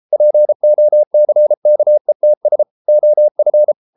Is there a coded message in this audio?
Yes: Morse code from 0.1 to 3.7 s, heard as a keyed tone, about 33 wpm, 600 Hz. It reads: POCKETS OF